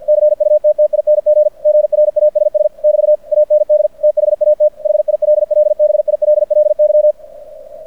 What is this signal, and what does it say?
Morse code, heard as a keyed tone, about 50 wpm, 600 Hz. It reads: CUTTING GRASS <BT> AND THAT HILLBILLY
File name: archived / CRSnet